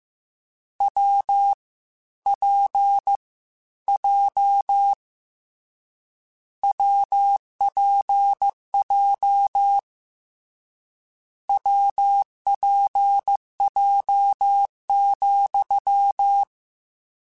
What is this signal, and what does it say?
Morse code, heard as a keyed tone, about 15 wpm, 800 Hz. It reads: W P J WPJ WPJ,
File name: Difficult Combos / Difficult Combos-mp3s / WPJ WPJ1 @13